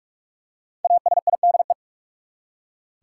Morse code, heard as a keyed tone, about 46 wpm, 700 Hz. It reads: ASIDE
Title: Five Letter Words - 45wpm